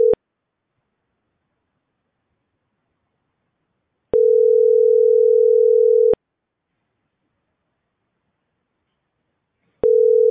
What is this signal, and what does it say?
Ringback tone (North America), 440 + 480 Hz.